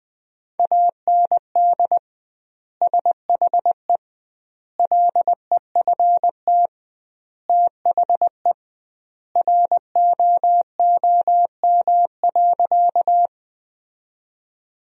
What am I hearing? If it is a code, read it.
Morse code, heard as a keyed tone, about 20 wpm, 700 Hz. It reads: AND SHE LEFT THE ROOM.
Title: 2x Word-Spacing - Sentences from Top 300 Words - 20wpm